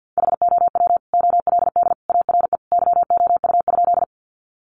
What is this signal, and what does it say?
Morse code, heard as a keyed tone, about 48 wpm, 700 Hz. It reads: HOW OLD ARE YOU?
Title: Sentences from Top 200 Words 50wpm